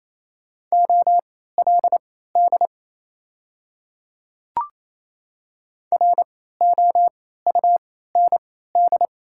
Morse code, heard as a keyed tone, about 28 wpm, spaced as for 16 wpm, 700 Hz. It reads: OLD ROUND
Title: 3x Character Spacing - Top 200 Words - 28wpm